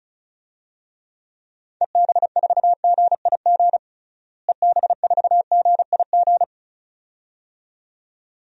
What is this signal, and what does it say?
Morse code, heard as a keyed tone, about 36 wpm, 700 Hz. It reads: EB4GIG EB4GIG
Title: DXCC-100 Call Signs - 6 Characters - Repeated 2x - 35wpm